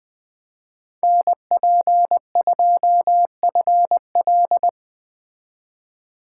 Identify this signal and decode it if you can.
Morse code, heard as a keyed tone, about 20 wpm, 700 Hz. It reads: NP2FL
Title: Speed Racing - USA Call Signs 17wpm